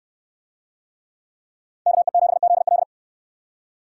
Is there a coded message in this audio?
Yes: Morse code from 1.9 to 2.8 s, heard as a keyed tone, about 63 wpm, 690 Hz. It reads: KE7ZL